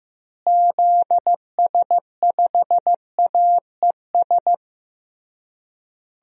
Morse code, heard as a keyed tone, about 15 wpm, 700 Hz. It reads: ZS5AES
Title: DXCC-100 Call Signs - 6 Characters - Repeated 3x - 15wpm